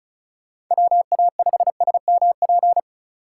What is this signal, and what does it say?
Morse code, heard as a keyed tone, about 36 wpm, 700 Hz. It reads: WA5SMP
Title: USA Call Signs - 6 Characters - 2x Character Spacing - 35wpm